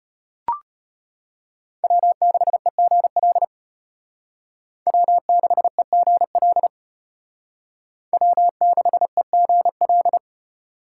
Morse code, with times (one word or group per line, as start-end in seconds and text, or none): 1.8-3.4 W6EGL
4.9-6.7 W6EGL
8.1-10.2 W6EGL